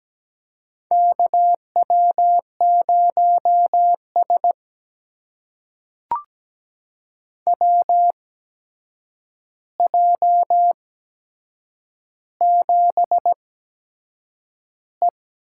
Morse code, heard as a keyed tone, about 17 wpm, spaced as for 7 wpm, 700 Hz. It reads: KW0S WJ7E
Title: USA Call Signs - 8x Character Spacing - 17wpm